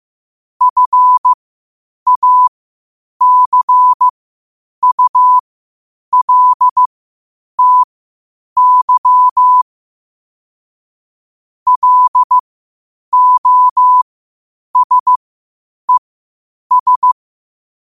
Morse code, heard as a keyed tone, about 15 wpm, spaced as for 8 wpm, 1000 Hz. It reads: FACULTY LOSES
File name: qotdfiles / 8wpm